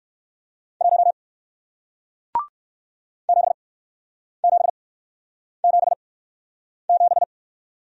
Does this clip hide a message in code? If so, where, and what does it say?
Yes: Morse code from 0.8 to 7.2 s, heard as a keyed tone, about 61 wpm, spaced as for 9 wpm, 700 Hz. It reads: 1 7777